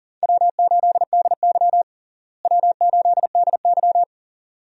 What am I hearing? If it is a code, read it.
Morse code, heard as a keyed tone, about 40 wpm, 700 Hz. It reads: W8DY W8DY
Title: USA Call Signs - 4 Characters - Repeated 2x - 40wpm